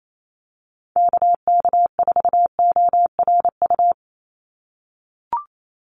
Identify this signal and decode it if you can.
Morse code, heard as a keyed tone, about 28 wpm, 700 Hz. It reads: KK4ORU
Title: USA Call Signs 28wpm